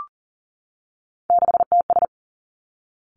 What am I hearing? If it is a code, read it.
Morse code, heard as a keyed tone, about 42 wpm, 700 Hz. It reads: 6TS
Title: Three Challenging Letter-Number Combinations 40wpm